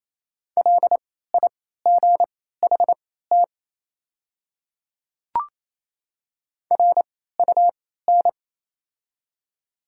Morse code, heard as a keyed tone, about 28 wpm, spaced as for 16 wpm, 700 Hz. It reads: LIGHT RUN
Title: 3x Character Spacing - Top 300 Words - 28wpm